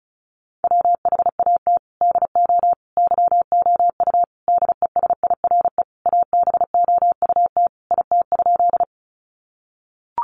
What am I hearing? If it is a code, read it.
Morse code, heard as a keyed tone, about 34 wpm, 700 Hz. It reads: WHAT DO YOU DESIRE ABOUT IT?
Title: Sentences from Top 1100 Words 35wpm